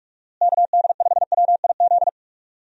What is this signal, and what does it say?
Morse code, heard as a keyed tone, about 42 wpm, 700 Hz. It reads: KD5WIZ
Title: USA Call Signs - 6 Characters - Repeated 2x - 45wpm